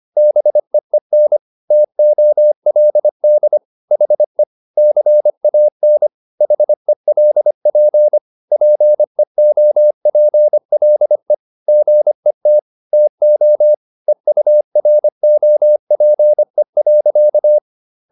Morse code, heard as a keyed tone, about 25 wpm, 600 Hz. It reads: BEEN TOLD HE CAN HELP PEOPLE GET TO EUROPE.